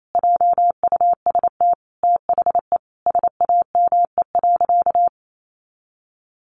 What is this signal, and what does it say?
Morse code, heard as a keyed tone, about 28 wpm, 700 Hz. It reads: JUST THE SAME.
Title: Sentences from Top 300 Words 28wpm